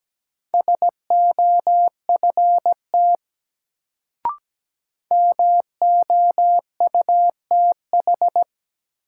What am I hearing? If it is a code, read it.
Morse code, heard as a keyed tone, about 17 wpm, 700 Hz. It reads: SOFT MOUTH
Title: Sentences from Top 1500 Words - New Word Drill Alt - 17wpm